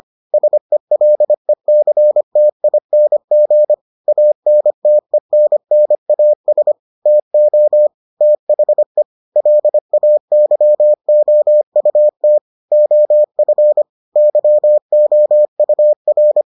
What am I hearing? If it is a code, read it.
Morse code, heard as a keyed tone, about 25 wpm, 600 Hz. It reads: SELECTING ANTENNAS TO THE LAYOUT OF YOUR